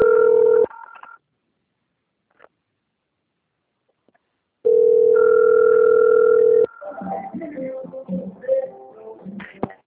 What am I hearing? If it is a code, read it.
Ringback tone (North America), 440 + 480 Hz.